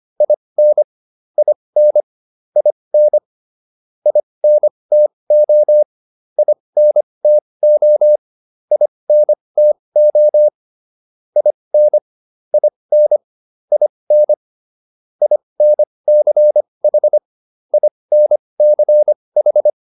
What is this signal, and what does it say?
Morse code, heard as a keyed tone, about 25 wpm, spaced as for 20 wpm, 600 Hz. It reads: IN IN IN INTO INTO INTO IN IN IN INCH INCH